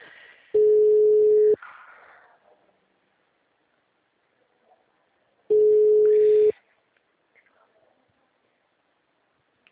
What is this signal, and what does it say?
Ringback tone (Europe, 425 Hz).
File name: out- -2010- - - .82